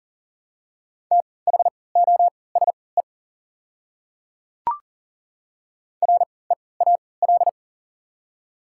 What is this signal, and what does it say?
Morse code, heard as a keyed tone, about 40 wpm, spaced as for 23 wpm, 700 Hz. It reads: THOSE REAL
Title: 3x Character Spacing - Top 300 Words - 40wpm